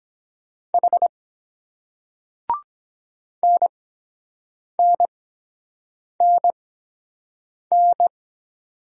Morse code, heard as a keyed tone, about 23 wpm, spaced as for 7 wpm, 700 Hz. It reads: H NNNN